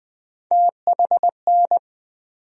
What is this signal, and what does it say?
Morse code, heard as a keyed tone, about 20 wpm, 700 Hz. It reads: THN
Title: USA Call Sign Postfixes - 3 Letters 20wpm